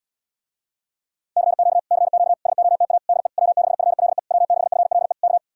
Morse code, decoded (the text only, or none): CQ CQ SOTA DE W6FLE W6FLE K